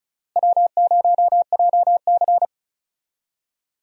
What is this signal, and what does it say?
Morse code, heard as a keyed tone, about 36 wpm, 700 Hz. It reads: W0JC
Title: USA Call Signs - 4 Characters - Repeated 3x - 35wpm